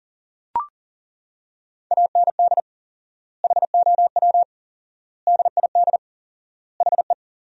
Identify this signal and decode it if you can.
Morse code, heard as a keyed tone, about 40 wpm, 700 Hz. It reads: AND HOW DID HE
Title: 4x Word-Spacing - Sentences from Top 400 Words - 40wpm